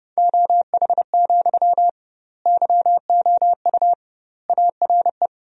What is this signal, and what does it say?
Morse code, heard as a keyed tone, about 30 wpm, 700 Hz. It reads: OH, YOU ARE
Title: 2x Word-Spacing - Sentences from Top 500 Words - 30wpm